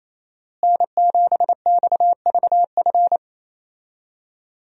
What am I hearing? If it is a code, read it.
Morse code, heard as a keyed tone, about 28 wpm, 700 Hz. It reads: N7XVF
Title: USA Call Signs - 5 Characters - Repeated 2x - 28wpm